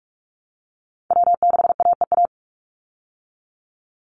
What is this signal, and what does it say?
Morse code, heard as a keyed tone, about 44 wpm, 700 Hz. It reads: W6AEA
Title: Call Signs Repeated 2x - 45wpm